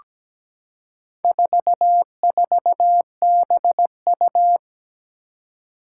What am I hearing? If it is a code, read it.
Morse code, heard as a keyed tone, about 17 wpm, 700 Hz. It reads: 44BU
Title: Four Challenging Letter-Number Combinations 17wpm